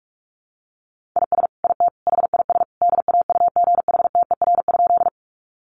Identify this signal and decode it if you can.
Morse code, heard as a keyed tone, about 46 wpm, 700 Hz. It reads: IS IT HIS DAUGHTER?
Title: Sentences from Top 1500 Words 45wpm